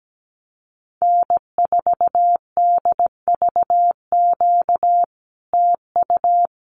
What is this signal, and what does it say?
Morse code, heard as a keyed tone, about 17 wpm, 700 Hz. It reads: N4DVQ TU
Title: Common QSO Phrases Encore 17wpm